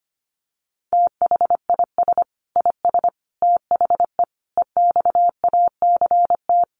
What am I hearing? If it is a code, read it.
Morse code, heard as a keyed tone, about 25 wpm, 700 Hz. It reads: THIS IS THE EXACT
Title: Sentences from Top 1300 Words 25wpm